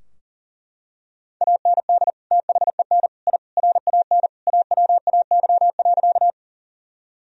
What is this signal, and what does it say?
Morse code, heard as a keyed tone, about 37 wpm, 700 Hz. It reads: AND THEN I RAN AWAY.
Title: Sentences from Top 1000 Words - Review Part 3 - 40wpm